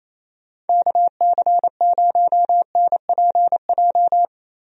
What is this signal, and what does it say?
Morse code, heard as a keyed tone, about 28 wpm, 700 Hz. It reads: KC0NPJ
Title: USA Call Signs - 6 Characters - Repeated 3x - 28wpm